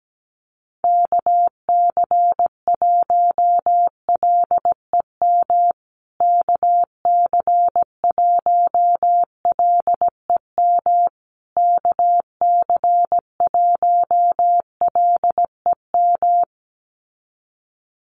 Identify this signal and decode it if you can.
Morse code, heard as a keyed tone, about 17 wpm, 700 Hz. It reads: KC1LEM KC1LEM KC1LEM
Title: USA Call Signs Repeated 3x 17wpm